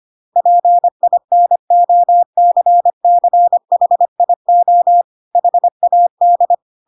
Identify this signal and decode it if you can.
Morse code, heard as a keyed tone, about 25 wpm, 700 Hz. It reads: PINOCCHIO HAD